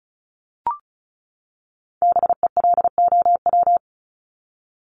Morse code, heard as a keyed tone, about 34 wpm, 700 Hz. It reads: BELOW
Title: Top 501-750 Words 35wpm